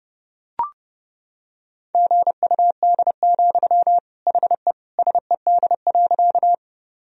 Morse code, heard as a keyed tone, about 30 wpm, 700 Hz. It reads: GUD, HE SED.
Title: Sentences from Top 200 Words and Abbreviations 30wpm